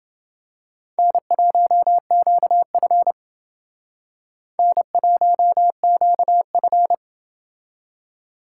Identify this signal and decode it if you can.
Morse code, heard as a keyed tone, about 29 wpm, 700 Hz. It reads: N1QF N1QF